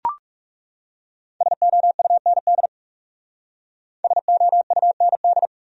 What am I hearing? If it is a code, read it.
Morse code, heard as a keyed tone, about 37 wpm, 700 Hz. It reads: SOUND SOUND
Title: Speed-Racing Top 500 Words 30wpm